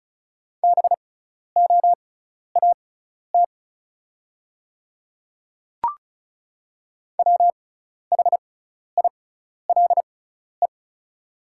Morse code, heard as a keyed tone, about 36 wpm, spaced as for 12 wpm, 690 Hz. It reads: BOAT WHILE
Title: 6x Character Spacing - Top 500 Words - 35wpm